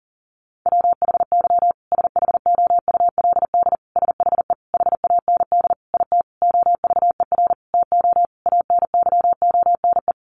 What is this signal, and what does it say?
Morse code, heard as a keyed tone, about 37 wpm, 700 Hz. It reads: WHY SHOULD SHE HAND IT OVER TO ANYONE
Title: Sentences from Top 900 Words 40wpm